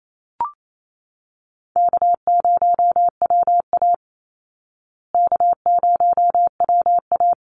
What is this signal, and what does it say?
Morse code, heard as a keyed tone, about 28 wpm, 700 Hz. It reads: K0WA K0WA
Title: Easy Call Signs Repeated 2x - 28wpm